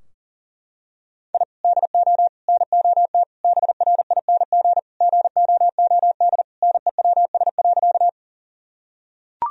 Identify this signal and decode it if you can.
Morse code, heard as a keyed tone, about 40 wpm, 700 Hz. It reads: I DO NOT BRING GOOD NEWS.
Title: Sentences from Top 1000 Words - Review Part 3 - 40wpm